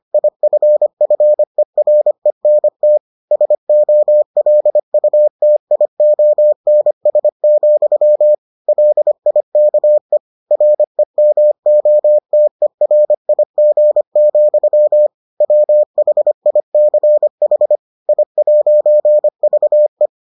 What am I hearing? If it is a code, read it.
Morse code, heard as a keyed tone, about 25 wpm, 600 Hz. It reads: IFFERENT SOLUTIONS, LIKE REMOTERIG, WHICH I'VE